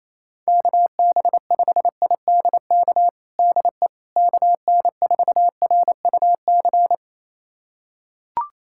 Morse code, heard as a keyed tone, about 28 wpm, 700 Hz. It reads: KB5IDK DE KN4RUC
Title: USA Callsign DE Callsign - 6 Characters - 28wpm